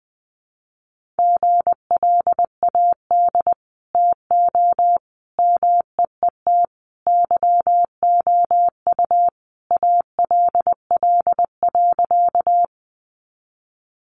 Morse code, heard as a keyed tone, about 19 wpm, 700 Hz. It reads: GLAD TO MEET YOU ALL.
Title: Sentences from Top 1100 Words 20wpm